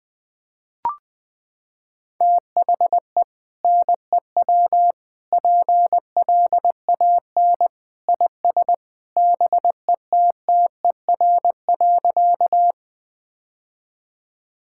Morse code, heard as a keyed tone, about 20 wpm, 700 Hz. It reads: THE NEW PLAN IS BETTER.
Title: Sentences with 2-Word Phrases - 20wpm